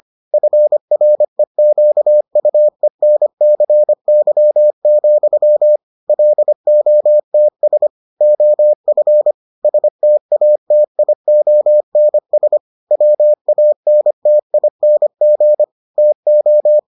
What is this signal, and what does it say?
Morse code, heard as a keyed tone, about 25 wpm, 600 Hz. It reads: FREQUENCY, LOTS OF STATIONS WANTING TO